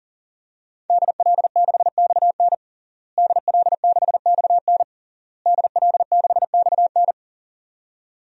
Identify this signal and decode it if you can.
Morse code, heard as a keyed tone, about 40 wpm, 700 Hz. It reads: DL6XN DL6XN DL6XN
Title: DXCC-100 Call Signs - 5 Characters - Repeated 3x - 40wpm